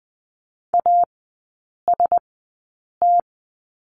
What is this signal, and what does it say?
Morse code, heard as a keyed tone, about 20 wpm, spaced as for 8 wpm, 700 Hz. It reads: AST
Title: 041 Lesson 8e - Number 1 - Three Characters 20wpm